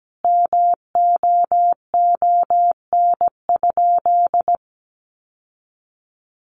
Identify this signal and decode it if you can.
Morse code, heard as a keyed tone, about 17 wpm, 700 Hz. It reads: MOON?
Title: Sentences from Top 1200 Words 17wpm